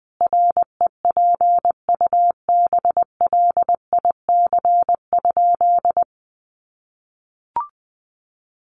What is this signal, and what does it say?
Morse code, heard as a keyed tone, about 20 wpm, 700 Hz. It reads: REPUBLIC?
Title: Sentences from Top 1500 Words 20wpm